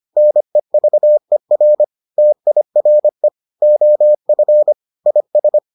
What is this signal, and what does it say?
Morse code, heard as a keyed tone, about 25 wpm, 600 Hz. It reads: NEVER TIRE OF IS